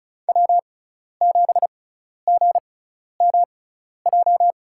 Morse code, heard as a keyed tone, about 34 wpm, spaced as for 12 wpm, 700 Hz. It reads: W7GMJ
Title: USA Call Signs - 5 Characters - 6x Character Spacing - 35wpm